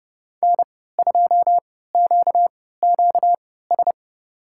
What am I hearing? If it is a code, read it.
Morse code, heard as a keyed tone, about 32 wpm, spaced as for 17 wpm, 690 Hz. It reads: N2QQS
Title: USA Call Signs - 3x Character Spacing - 30wpm